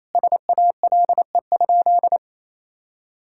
Morse code, heard as a keyed tone, about 28 wpm, 700 Hz. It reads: SALE?
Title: Sentences from Top 1600 Words 28wpm